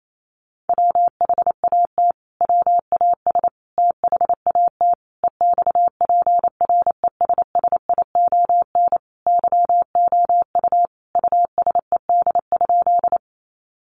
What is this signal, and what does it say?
Morse code, heard as a keyed tone, about 28 wpm, 700 Hz. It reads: WHAT WAS THAT EXPRESSION YOU USED?
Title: Sentences from Top 1200 Words 28wpm